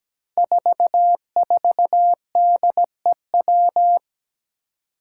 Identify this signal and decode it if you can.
Morse code, heard as a keyed tone, about 17 wpm, 700 Hz. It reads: 44DEW
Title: Five Challenging Letter-Number Combinations 17wpm